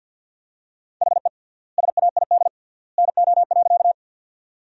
Morse code, heard as a keyed tone, about 50 wpm, 700 Hz. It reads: HE SAID NO.
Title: 3x Word-Spacing - Sentences from Top 100 Words - 50wpm